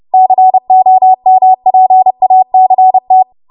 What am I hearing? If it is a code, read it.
Morse code, heard as a keyed tone, about 30 wpm, 750 Hz. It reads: COMPACT